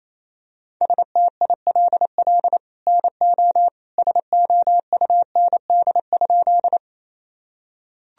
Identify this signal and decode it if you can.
Morse code, heard as a keyed tone, about 28 wpm, 700 Hz. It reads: STILL NO SOUND?
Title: Sentences from Top 1000 Words - Review Part 1 - 28wpm